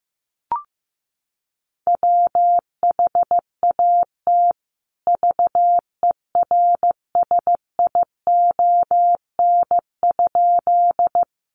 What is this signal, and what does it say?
Morse code, heard as a keyed tone, about 15 wpm, 700 Hz. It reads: WHAT VERSION?
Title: Sentences from Top 1400 Words 15wpm